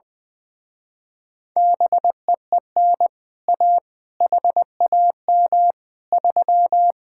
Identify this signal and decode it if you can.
Morse code, heard as a keyed tone, about 21 wpm, 700 Hz. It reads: BEEN A HAM 3
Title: Speed-Racing Common QSO Phrases Encore - 20wpm